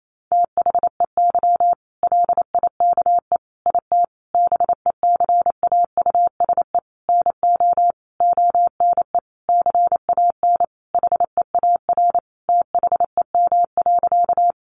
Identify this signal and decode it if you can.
Morse code, heard as a keyed tone, about 28 wpm, 700 Hz. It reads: THEY LIKE IT BECAUSE NO ONE CAN HEAR THEM.
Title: Sentences from Top 600 Words 28wpm